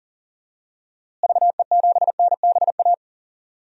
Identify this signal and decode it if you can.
Morse code, heard as a keyed tone, about 40 wpm, 700 Hz. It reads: VE7NBA
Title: DXCC-100 Call Signs - 6 Characters - Repeated 2x - 40wpm